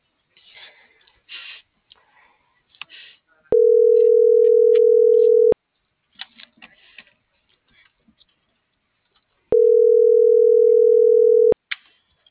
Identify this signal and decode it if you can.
Ringback tone (North America), 440 + 480 Hz.